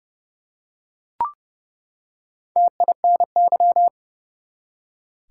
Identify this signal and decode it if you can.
Morse code, heard as a keyed tone, about 30 wpm, 700 Hz. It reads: TINY
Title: Four Letter Words - 30wpm